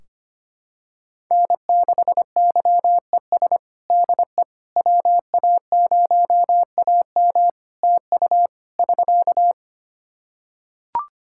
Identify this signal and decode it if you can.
Morse code, heard as a keyed tone, about 25 wpm, 700 Hz. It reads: N6YES DE WA0AM TU <SK>
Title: Practice QSO - Exchange 4 - Ending - 25wpm